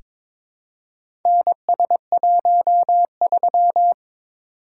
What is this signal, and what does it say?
Morse code, heard as a keyed tone, about 22 wpm, 700 Hz. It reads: NS13
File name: practice / 22wpm / Four Letter-Number Combinations - 22wpm